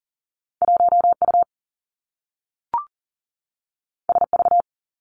Morse code, heard as a keyed tone, about 42 wpm, 700 Hz. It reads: 1U SV
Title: Two Challenging Letter-Number Combinations 40wpm